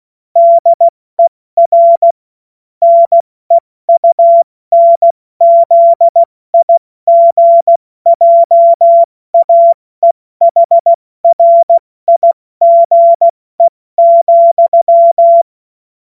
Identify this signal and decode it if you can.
Morse code, heard as a keyed tone, about 16 wpm, 680 Hz. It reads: DER NEUNZIGJAEHRIGE,
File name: Morsekurs / Witze / Witz 04. Januar 2022, 70 F80